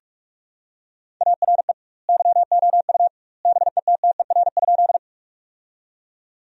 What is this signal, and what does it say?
Morse code, heard as a keyed tone, about 41 wpm, 700 Hz. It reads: ARE YOU BETTER?
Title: 2x Word-Spacing - Sentences from Top 400 Words - 45wpm